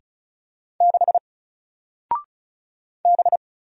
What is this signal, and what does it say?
Morse code, heard as a keyed tone, about 34 wpm, 700 Hz. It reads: 6 B
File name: Single-Letter-Number-35wpm